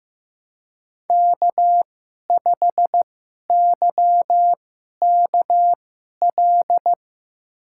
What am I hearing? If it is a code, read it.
Morse code, heard as a keyed tone, about 15 wpm, spaced as for 11 wpm, 700 Hz. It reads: K5YKL
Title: USA Call Signs - 5 Characters - 2x Character Spacing - 15wpm